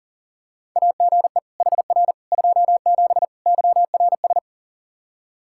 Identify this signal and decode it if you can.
Morse code, heard as a keyed tone, about 37 wpm, 700 Hz. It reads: AGE HR 27 YRS
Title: Practice QSO - Exchange 3 - More Info - 40wpm